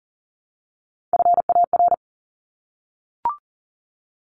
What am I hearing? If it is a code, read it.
Morse code, heard as a keyed tone, about 40 wpm, 700 Hz. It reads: FAR
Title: Top 500 Words Repeated 40wpm